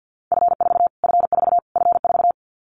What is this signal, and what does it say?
Morse code, heard as a keyed tone, about 50 wpm, 700 Hz. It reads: F4 F4 F4
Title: Callsign Prefixes for Top 25 DXCC - Repeated 3x - 50wpm